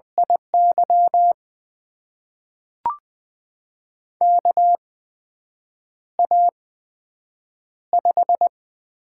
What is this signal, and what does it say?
Morse code, heard as a keyed tone, about 20 wpm, spaced as for 7 wpm, 700 Hz. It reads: IY KA5
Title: USA Call Signs - 6 Characters - 8x Character Spacing - 20wpm